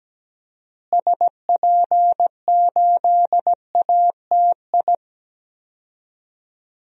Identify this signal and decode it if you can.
Morse code, heard as a keyed tone, about 17 wpm, 700 Hz. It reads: SP8ATI
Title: DXCC-100 Call Signs - 6 Characters - Repeated 3x - 17wpm